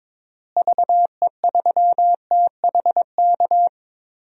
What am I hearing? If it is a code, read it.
Morse code, heard as a keyed tone, about 22 wpm, 700 Hz. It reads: VE3THK
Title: DXCC-100 Call Signs - 6 Characters - Repeated 3x - 22wpm